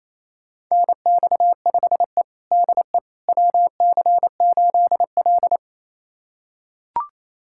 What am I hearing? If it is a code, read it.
Morse code, heard as a keyed tone, about 28 wpm, 700 Hz. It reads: NX5E DE WC8L
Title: USA Callsign DE Callsign - 4 Characters - 28wpm